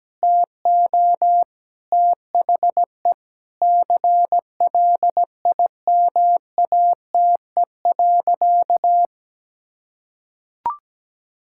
Morse code, heard as a keyed tone, about 17 wpm, 700 Hz. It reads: TO THE CLIMATE.
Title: Sentences from Top 1600 Words 17wpm